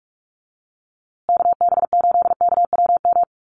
Morse code, heard as a keyed tone, about 44 wpm, 700 Hz. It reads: KB8KWM
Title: USA Call Signs Repeated 3x 45wpm